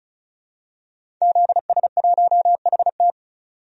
Morse code, heard as a keyed tone, about 36 wpm, 700 Hz. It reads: ZS1HT
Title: DXCC-100 Call Signs - 5 Characters - 35wpm